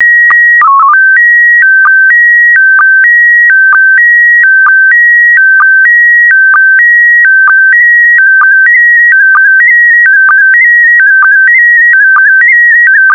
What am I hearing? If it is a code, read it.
SSTV (slow-scan television), PD 290.